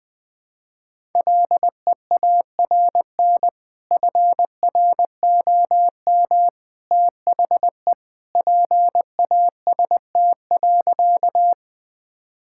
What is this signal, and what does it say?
Morse code, heard as a keyed tone, about 20 wpm, 700 Hz. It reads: LEARN FROM THE PAST.
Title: Sentences with 2-Word Phrases - 20wpm